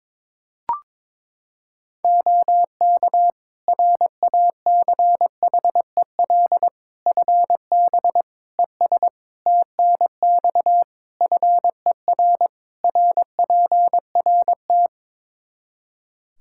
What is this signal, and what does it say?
Morse code, heard as a keyed tone, about 22 wpm, 700 Hz. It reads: OK RACHEL FB ES TNX FER RPRT
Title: Practice QSO - Exchange 2 - Info - 22wpm